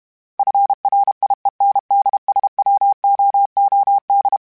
Morse code, heard as a keyed tone, about 32 wpm, 800 Hz. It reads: FRIENDSWOOD
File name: fromSMs / FromSMs mp3s / Cities@30